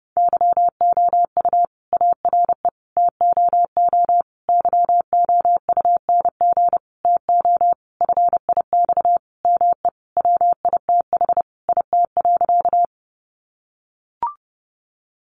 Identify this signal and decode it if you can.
Morse code, heard as a keyed tone, about 30 wpm, 700 Hz. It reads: YOU ARE TOO YOUNG TO FIX ME WITH IT.
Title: Sentences from Top 1400 Words 30wpm